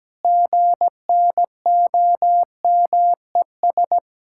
Morse code, heard as a keyed tone, about 17 wpm, 700 Hz. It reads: GNOMES
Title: Pangram Challenge v2 - 17wpm